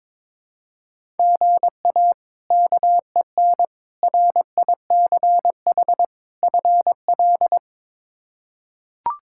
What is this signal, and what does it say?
Morse code, heard as a keyed tone, about 21 wpm, 700 Hz. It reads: GA KEN RICH FL
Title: K1USN SST QSO Practice - 22wpm